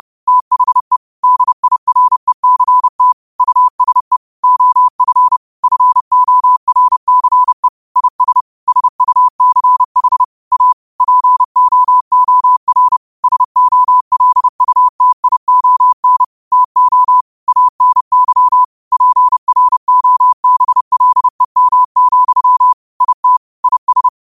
Morse code, heard as a keyed tone, about 30 wpm, 1000 Hz. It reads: THE DIRECT USE OF FORCE IS SUCH A POOR SOLUTION TO ANY PROBLEM, IT IS